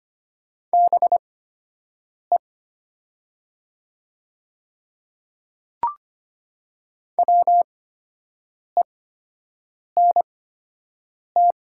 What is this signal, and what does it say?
Morse code, heard as a keyed tone, about 25 wpm, spaced as for 7 wpm, 700 Hz. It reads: BE WENT
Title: 8x Character Spacing - Top 300 Words - 25wpm